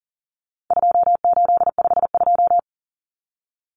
Morse code, heard as a keyed tone, about 40 wpm, 700 Hz. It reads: 2852
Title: Common Cut Numbers - 4-Digit 40wpm